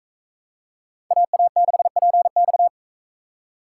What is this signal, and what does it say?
Morse code, heard as a keyed tone, about 38 wpm, 700 Hz. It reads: AA6PX